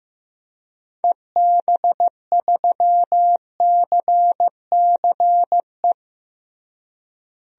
Morse code, heard as a keyed tone, about 15 wpm, 700 Hz. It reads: EB3CCE
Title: DXCC-100 Call Signs - 6 Characters - Repeated 3x - 15wpm